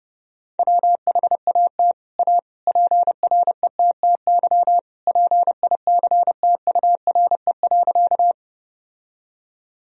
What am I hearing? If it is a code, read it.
Morse code, heard as a keyed tone, about 30 wpm, 700 Hz. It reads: WHAT A PRETTY PICTURE.
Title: Sentences from Top 1000 Words - Review Part 1 - 30wpm